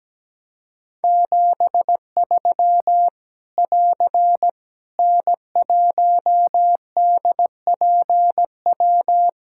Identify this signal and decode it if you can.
Morse code, heard as a keyed tone, about 17 wpm, 700 Hz. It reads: 73 <AR> N1DPW
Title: K1USN SST QSO Practice 17wpm